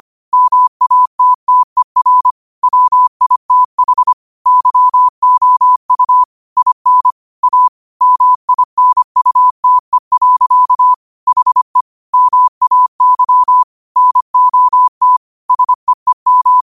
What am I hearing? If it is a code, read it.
Morse code, heard as a keyed tone, about 25 wpm, 1000 Hz. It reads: MATTER WITH YOU IN A MINUTE. HE MAY NOT SEEM
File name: qotdfiles / 25wpm